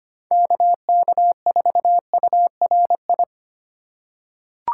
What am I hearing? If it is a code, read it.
Morse code, heard as a keyed tone, about 25 wpm, 700 Hz. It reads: KK4URI
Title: USA Call Signs - 6 Characters - 4x Character Spacing - 25wpm